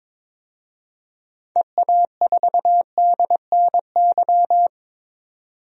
Morse code, heard as a keyed tone, about 22 wpm, 700 Hz. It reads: EA4DNY
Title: DXCC-100 Call Signs - 6 Characters - 22wpm